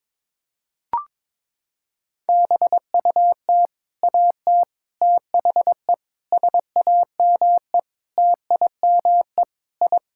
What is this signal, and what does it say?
Morse code, heard as a keyed tone, about 22 wpm, 700 Hz. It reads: BUT AT THE SAME TIME I
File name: practice / 22wpm / Sentences with 4-Word Phrases - 22wpm